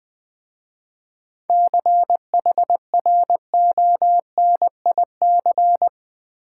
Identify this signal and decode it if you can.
Morse code, heard as a keyed tone, about 20 wpm, 700 Hz. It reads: CHRONIC
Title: 7 Letter Words - First 3 Letters Given - 20wpm